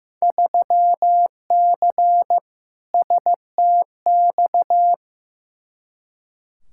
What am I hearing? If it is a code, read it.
Morse code, heard as a keyed tone, about 15 wpm, 700 Hz. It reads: 3C STX
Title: Field Day Exchanges Encore - 15wpm